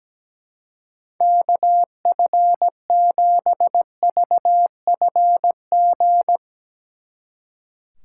Morse code, heard as a keyed tone, about 18 wpm, 700 Hz. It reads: KF7VFG
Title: Speed Racing - USA 2x3 Call Signs 17wpm